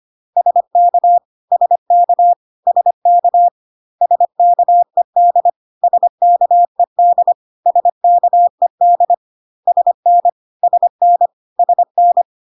Morse code, heard as a keyed tone, about 25 wpm, 700 Hz. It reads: SK SK SK SKED SKED SKED SN SN SN
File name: CW abbreviations alphabetical repeated 3 times 25 25wpm ik1hge